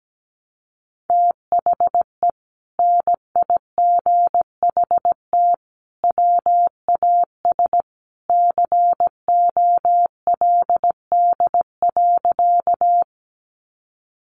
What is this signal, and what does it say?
Morse code, heard as a keyed tone, about 17 wpm, 700 Hz. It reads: THE NIGHT WAS COLD.
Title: Sentences from Top 400 Words 17wpm